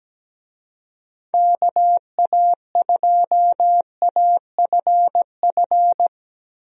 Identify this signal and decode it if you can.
Morse code, heard as a keyed tone, about 17 wpm, 700 Hz. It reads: KA2AFF
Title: USA Call Signs - 6 Characters - Repeated 3x - 17wpm